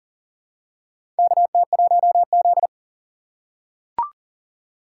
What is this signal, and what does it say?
Morse code, heard as a keyed tone, about 40 wpm, 700 Hz. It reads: KT1Z
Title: USA Call Signs - 4 Characters - 3x Character Spacing - 40wpm